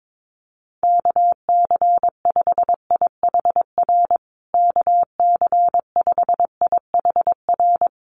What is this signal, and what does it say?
Morse code, heard as a keyed tone, about 22 wpm, 700 Hz. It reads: KC5IHR KC5IHR
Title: USA Call Signs Repeated 3x 22wpm